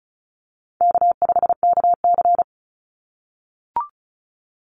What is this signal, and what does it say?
Morse code, heard as a keyed tone, about 34 wpm, 700 Hz. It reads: K5KC
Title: Call Signs - 35wpm